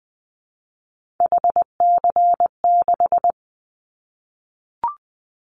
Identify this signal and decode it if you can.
Morse code, heard as a keyed tone, about 20 wpm, 700 Hz. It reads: HC6
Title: Callsign Prefixes for Top 100 DXCC - 20wpm